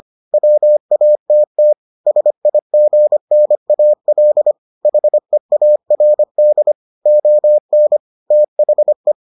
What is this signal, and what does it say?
Morse code, heard as a keyed tone, about 25 wpm, 600 Hz. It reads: WATT SIGNAL HEARD ON THE